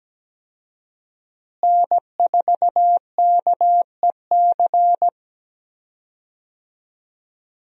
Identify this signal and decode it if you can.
Morse code, heard as a keyed tone, about 17 wpm, 700 Hz. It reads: N4KEC
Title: USA Call Signs - 5 Characters - Repeated 3x - 17wpm